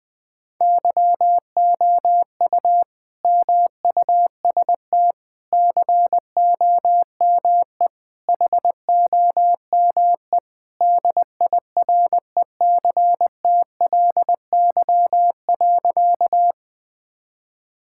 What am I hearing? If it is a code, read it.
Morse code, heard as a keyed tone, about 19 wpm, 700 Hz. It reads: YOU MUST COME HOME DIRECTLY.
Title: Sentences from Top 1000 Words - Review Part 5 - 20wpm